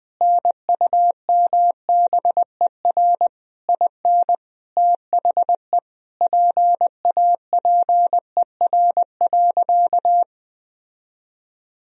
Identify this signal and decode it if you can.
Morse code, heard as a keyed tone, about 20 wpm, 700 Hz. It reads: NUMBER IN THE PAPER.
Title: Sentences from Top 1000 Words - Review Part 1 - 20wpm